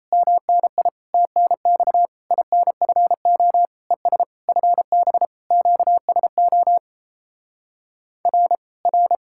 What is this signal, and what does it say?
Morse code, heard as a keyed tone, about 37 wpm, 700 Hz. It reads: MNI TNX INFO ES FB QSO R R